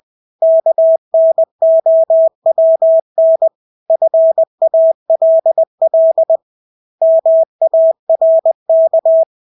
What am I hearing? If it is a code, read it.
Morse code, heard as a keyed tone, about 20 wpm, 650 Hz. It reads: KNOWN FALL MARK
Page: Engelske ord 20 wpm